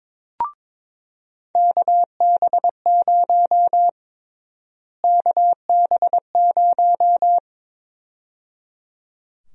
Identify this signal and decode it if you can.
Morse code, heard as a keyed tone, about 22 wpm, 700 Hz. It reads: KB0 KB0
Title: USA Call Sign Prefixes - Repeated 2x - 22wpm